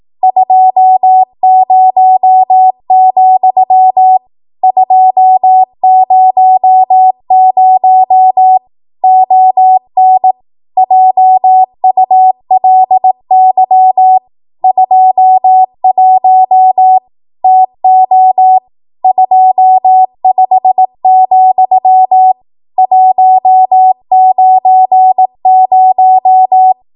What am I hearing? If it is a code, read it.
Morse code, heard as a keyed tone, about 18 wpm, 750 Hz. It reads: 20, 200 ON JULY 21 TO 25, 190
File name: Archive / 18 WPM / 18WPM